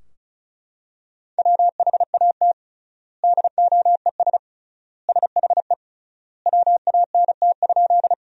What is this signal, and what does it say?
Morse code, heard as a keyed tone, about 36 wpm, 700 Hz. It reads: WHAT DOES SHE WANT?
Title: 3x Word-Spacing - Sentences from Top 200 Words - 35wpm